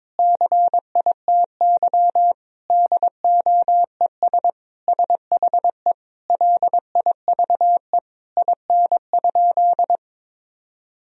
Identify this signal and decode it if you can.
Morse code, heard as a keyed tone, about 22 wpm, 700 Hz. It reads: CITY DOES SHE LIVE IN?
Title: Sentences from Top 1000 Words - Review Part 2 - 22wpm